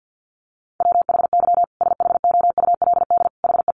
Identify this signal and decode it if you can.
Morse code, heard as a keyed tone, about 49 wpm, 700 Hz. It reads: WHY SHOULD HE
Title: Sentences from Top 1500 Words 50wpm